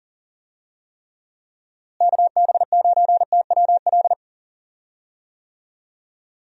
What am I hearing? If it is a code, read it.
Morse code, heard as a keyed tone, about 39 wpm, 700 Hz. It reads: KB9TWL
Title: USA Call Signs - 6 Characters - Repeated 2x - 40wpm